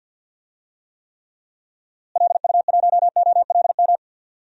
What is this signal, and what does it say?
Morse code, heard as a keyed tone, about 51 wpm, 700 Hz. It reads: LU1OLM
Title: DXCC-100 Call Signs - 6 Characters - Repeated 3x - 50wpm